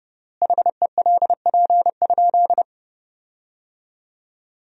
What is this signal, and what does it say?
Morse code, heard as a keyed tone, about 30 wpm, 700 Hz. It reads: HELP?
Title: 3x Word-Spacing - Sentences from Top 200 Words - 30wpm